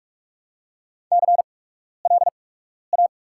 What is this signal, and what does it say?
Morse code, heard as a keyed tone, about 45 wpm, spaced as for 12 wpm, 700 Hz. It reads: CLA
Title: 8x Character Spacing - Top 500 Words - 45wpm